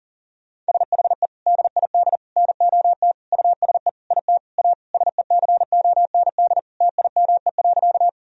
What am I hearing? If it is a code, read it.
Morse code, heard as a keyed tone, about 40 wpm, 700 Hz. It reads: SHE DID NOT USE IT A SECOND TIME.
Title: Sentences from Top 1000 Words - Review Part 1 - 40wpm